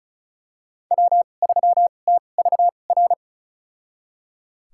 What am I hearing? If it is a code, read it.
Morse code, heard as a keyed tone, about 36 wpm, spaced as for 25 wpm, 700 Hz. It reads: W3TVR
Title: USA Call Signs - 2x Character Spacing - 35wpm